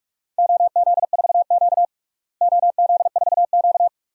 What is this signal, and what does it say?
Morse code, heard as a keyed tone, about 46 wpm, 700 Hz. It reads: OZ4Q OZ4Q
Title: DXCC-100 Call Signs - 4 Characters - Repeated 3x - 45wpm